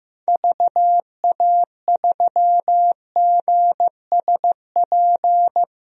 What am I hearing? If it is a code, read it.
Morse code, heard as a keyed tone, about 16 wpm, 700 Hz. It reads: VA3GSP
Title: DXCC-100 Call Signs - 6 Characters - 15wpm